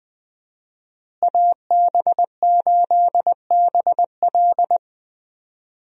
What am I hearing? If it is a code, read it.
Morse code, heard as a keyed tone, about 20 wpm, 700 Hz. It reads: AB8BL
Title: USA Call Signs - 5 Characters - Repeated 3x - 20wpm